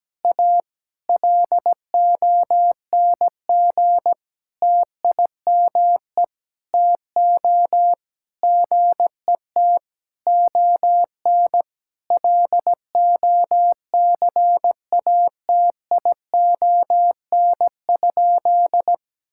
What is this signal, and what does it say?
Morse code, heard as a keyed tone, about 17 wpm, 700 Hz. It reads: A LONG TIME TO GET ON LOCATION?
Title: Sentences from Top 1000 Words - Review Part 6 - 17wpm